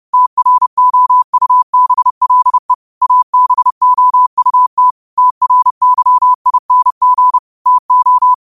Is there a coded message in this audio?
Yes: Morse code from 0.1 to 8.3 s, heard as a keyed tone, about 30 wpm, 1000 Hz. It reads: TROUBLE ABOUT TRYING TO